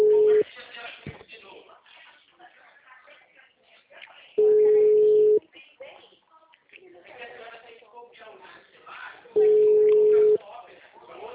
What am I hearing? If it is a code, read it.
Ringback tone (Europe, 425 Hz), with speech.
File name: out- -2017- - - .106